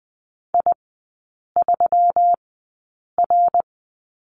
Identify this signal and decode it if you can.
Morse code, heard as a keyed tone, about 20 wpm, spaced as for 8 wpm, 700 Hz. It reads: I3R
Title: 121 Lesson 20e - Number 3 - Three Characters 20wpm